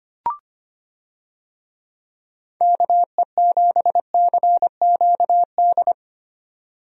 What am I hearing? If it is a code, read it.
Morse code, heard as a keyed tone, about 25 wpm, 700 Hz. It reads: KE7CQD
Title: USA Call Signs - 6 Characters - 25wpm